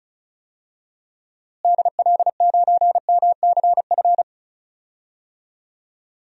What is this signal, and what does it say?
Morse code, heard as a keyed tone, about 36 wpm, 700 Hz. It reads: DL9MCF
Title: DXCC-100 Call Signs - 6 Characters - Repeated 3x - 35wpm